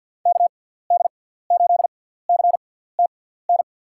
Morse code, heard as a keyed tone, about 52 wpm, spaced as for 17 wpm, 700 Hz. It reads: KD8CTN
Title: USA Call Signs - 6 Characters - 6x Character Spacing - 50wpm